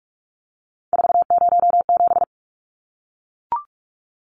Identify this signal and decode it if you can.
Morse code, heard as a keyed tone, about 44 wpm, 700 Hz. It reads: 407
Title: Numbers - 3-Digit 45wpm